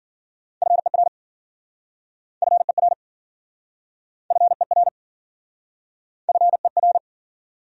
Morse code, heard as a keyed tone, about 52 wpm, 700 Hz. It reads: FER FER FER FER